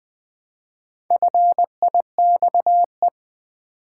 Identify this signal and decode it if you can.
Morse code, heard as a keyed tone, about 20 wpm, 700 Hz. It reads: FIXE
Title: Build A Word - 1 Letter At A Time - 5-Letter Words - 20wpm